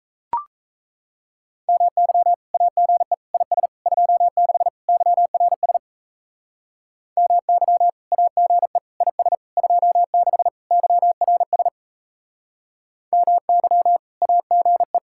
Morse code, heard as a keyed tone, about 34 wpm, 700 Hz. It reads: MY AGE IS 26 YRS MY AGE IS 26 YRS MY AGE
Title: Speed-Racing Common QSO Phrases - 28wpm